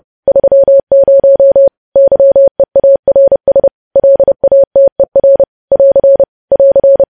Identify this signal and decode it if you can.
Morse code, heard as a keyed tone, about 30 wpm, 550 Hz. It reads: 30 YEARS LATER <AR> <AR>
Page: news headlines in morse code at 30 wpm